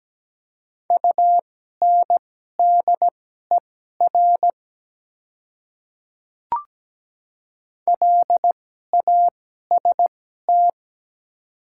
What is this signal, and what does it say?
Morse code, heard as a keyed tone, about 17 wpm, spaced as for 12 wpm, 700 Hz. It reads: UNDER LAST
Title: 2x Character Spacing - Top 300 Words - 17wpm